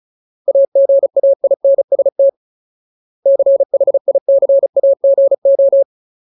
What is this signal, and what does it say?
Morse code, heard as a keyed tone, about 35 wpm, 550 Hz. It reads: AGAINST CHICAGO
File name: Ultimate Word List / 35WPM Ultimate